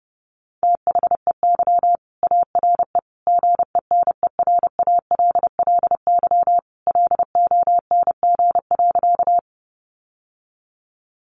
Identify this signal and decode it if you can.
Morse code, heard as a keyed tone, about 30 wpm, 700 Hz. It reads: THEY ARE GENERALLY LONG.
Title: Sentences from Top 600 Words 30wpm